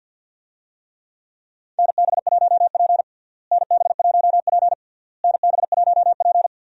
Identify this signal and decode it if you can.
Morse code, heard as a keyed tone, about 52 wpm, 700 Hz. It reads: NB1P NB1P NB1P
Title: USA Call Signs - 4 Characters - Repeated 3x - 50wpm